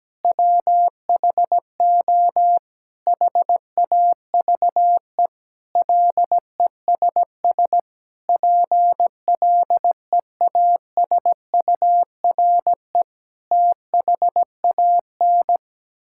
Morse code, heard as a keyed tone, about 17 wpm, 700 Hz. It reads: WHO HAVE LESS PLEASURE THAN
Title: Sentences from Top 1600 Words 17wpm